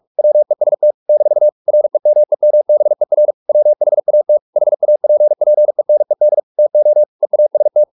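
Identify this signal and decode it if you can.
Morse code, heard as a keyed tone, about 44 wpm, 600 Hz. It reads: WEST - REMEMBER WHAT HAPPENED TO EAST
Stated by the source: Morse practice at 45 wpm, the novel Never by Ken Follett in English, from CW med Gnister.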